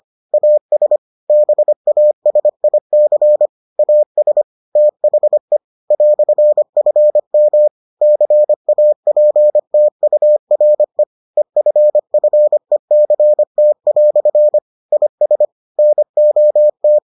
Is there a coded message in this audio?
Yes: Morse code from 0.3 to 17.0 s, heard as a keyed tone, about 25 wpm, 600 Hz. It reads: AS BASIC AS THE "FM CAPTURE EFFECT" IS NOT